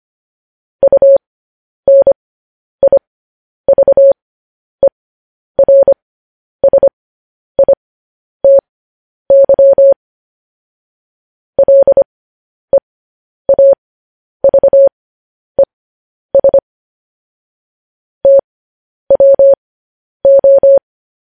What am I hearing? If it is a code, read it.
Morse code, heard as a keyed tone, about 25 wpm, spaced as for 10 wpm, 550 Hz. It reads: UNIVERSITY LEAVES TWO
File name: podcast / News headlines-1x-25x10